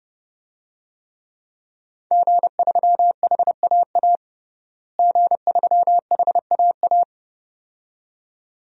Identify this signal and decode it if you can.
Morse code, heard as a keyed tone, about 30 wpm, 700 Hz. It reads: G3HAA G3HAA
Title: DXCC-100 Call Signs - 5 Characters - Repeated 2x - 30wpm